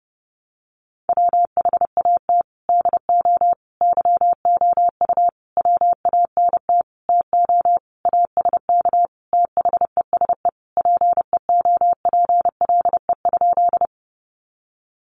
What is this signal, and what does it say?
Morse code, heard as a keyed tone, about 30 wpm, 700 Hz. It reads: WHAT DO YOU WANT TO ASK THESE PEOPLE?
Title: Sentences from Top 200 Words 30wpm